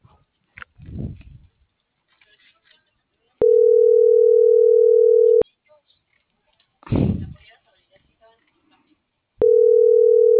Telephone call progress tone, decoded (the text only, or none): ringback tone